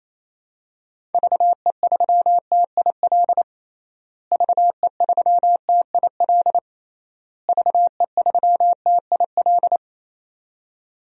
Morse code, heard as a keyed tone, about 28 wpm, 700 Hz. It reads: VE3TIL VE3TIL VE3TIL
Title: DXCC-100 Call Signs - 6 Characters - Repeated 3x - 28wpm